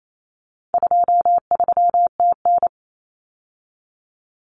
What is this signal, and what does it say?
Morse code, heard as a keyed tone, about 28 wpm, 700 Hz. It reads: 23TN
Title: Common Cut Numbers - 4-Digit 28wpm